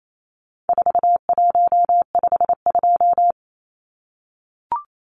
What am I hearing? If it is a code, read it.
Morse code, heard as a keyed tone, about 28 wpm, 700 Hz. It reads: 4152
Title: Numbers - 4-Digit 28wpm